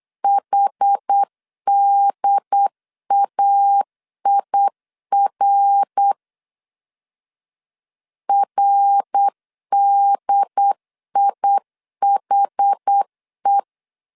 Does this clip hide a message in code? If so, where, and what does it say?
Yes: Morse code from 0.2 to 13.6 s, heard as a keyed tone, about 8 wpm, 800 Hz. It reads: HDAIR RDIHE